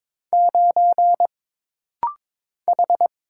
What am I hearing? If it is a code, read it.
Morse code, heard as a keyed tone, about 22 wpm, 700 Hz. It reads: 9 H
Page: This rapid-fire timing is geared to push your ICR (Instant Character Recognition) skills.